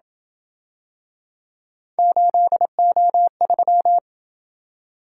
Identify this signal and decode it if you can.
Morse code, heard as a keyed tone, about 27 wpm, 700 Hz. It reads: 8O3